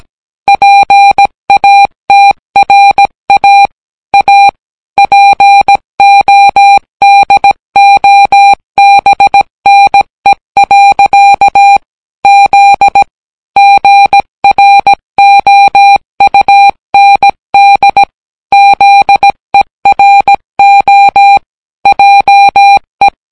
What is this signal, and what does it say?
Morse code, heard as a keyed tone, about 17 wpm, 800 Hz. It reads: PATRA A PODOBNE. Z GROUND ZERO JE